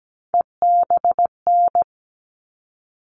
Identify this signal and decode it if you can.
Morse code, heard as a keyed tone, about 17 wpm, 700 Hz. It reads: EBN
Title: All Cut Numbers - 3-Digit 17wpm